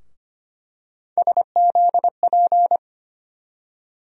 Morse code, heard as a keyed tone, about 25 wpm, 700 Hz. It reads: SZP
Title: USA Call Sign Postfixes - 3 Letters 25wpm